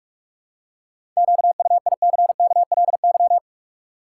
Morse code, heard as a keyed tone, about 46 wpm, 700 Hz. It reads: QUICKLY
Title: Seven Letter Words - 45wpm